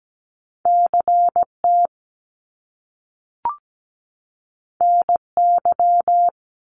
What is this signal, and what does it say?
Morse code, heard as a keyed tone, about 17 wpm, 700 Hz. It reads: CT NY
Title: Top Two Letter Combinations in Words 17wpm